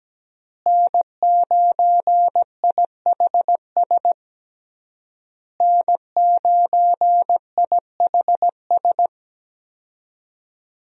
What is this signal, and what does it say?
Morse code, heard as a keyed tone, about 17 wpm, 700 Hz. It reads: N9IHS N9IHS
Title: USA Call Signs - 5 Characters - Repeated 3x - 17wpm